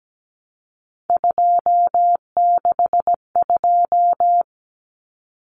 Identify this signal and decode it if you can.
Morse code, heard as a keyed tone, about 17 wpm, 700 Hz. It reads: 262
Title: Numbers - 3-Digit 17wpm